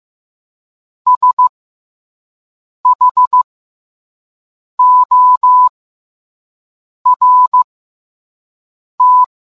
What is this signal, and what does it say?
Morse code, heard as a keyed tone, about 15 wpm, spaced as for 5 wpm, 1000 Hz. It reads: SHORT